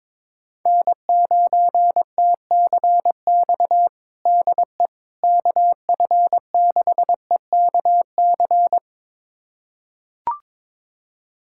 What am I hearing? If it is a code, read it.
Morse code, heard as a keyed tone, about 22 wpm, 700 Hz. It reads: N9TCX DE KF6EKC
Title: USA Callsign DE Callsign 22wpm